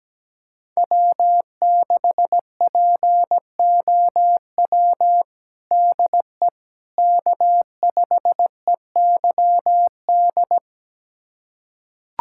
Morse code, heard as a keyed tone, about 17 wpm, 700 Hz. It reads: W6POW DE K5EYD
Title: USA Callsign DE Callsign 17wpm